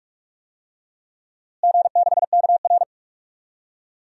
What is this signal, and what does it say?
Morse code, heard as a keyed tone, about 47 wpm, 700 Hz. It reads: G6KR
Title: DXCC-100 Call Signs - 4 Characters - Repeated 2x - 45wpm